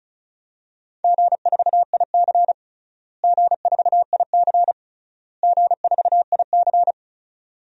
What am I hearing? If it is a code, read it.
Morse code, heard as a keyed tone, about 36 wpm, 700 Hz. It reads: G4IC G4IC G4IC
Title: DXCC-100 Call Signs - 4 Characters - Repeated 3x - 35wpm